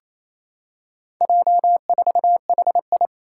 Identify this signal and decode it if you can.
Morse code, heard as a keyed tone, about 28 wpm, 700 Hz. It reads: J4HI
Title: Four Challenging Letter-Number Combinations 28wpm